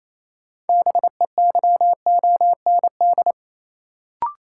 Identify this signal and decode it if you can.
Morse code, heard as a keyed tone, about 28 wpm, 700 Hz. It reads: BEYOND
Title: Sentences from Top 1000 Words - Review Part 6 - New Word Drill - 28wpm